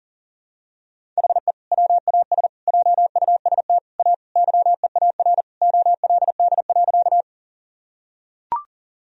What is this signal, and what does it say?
Morse code, heard as a keyed tone, about 40 wpm, 700 Hz. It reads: HE WAS JUST A YEAR OLD.
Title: Sentences from Top 1000 Words - Review Part 1 - 40wpm